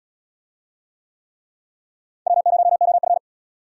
Morse code, heard as a keyed tone, about 75 wpm, 700 Hz. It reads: W0OF